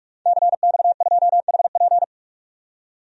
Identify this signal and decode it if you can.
Morse code, heard as a keyed tone, about 46 wpm, 700 Hz. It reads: CX2HP
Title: DXCC-100 Call Signs - 5 Characters - 45wpm